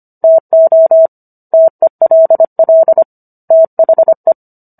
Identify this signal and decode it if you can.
Morse code, heard as a keyed tone, about 25 wpm, 650 Hz. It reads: TO TELL THE